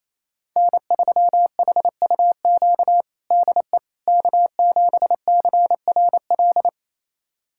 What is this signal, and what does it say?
Morse code, heard as a keyed tone, about 28 wpm, 700 Hz. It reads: N3HUQ DE K7CRL
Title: USA Callsign DE Callsign - 5 Characters - 28wpm